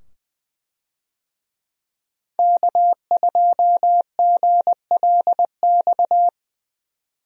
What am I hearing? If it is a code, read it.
Morse code, heard as a keyed tone, about 21 wpm, 700 Hz. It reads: K2GLX
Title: USA Call Signs - 5 Characters - 20wpm